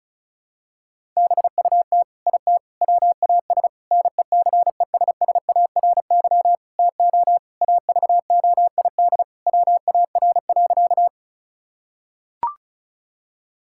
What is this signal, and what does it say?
Morse code, heard as a keyed tone, about 36 wpm, 700 Hz. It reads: BUT IT WAS NECESSARY TO AVOID WAR.
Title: Sentences from Top 1000 Words - Review Part 6 - 35wpm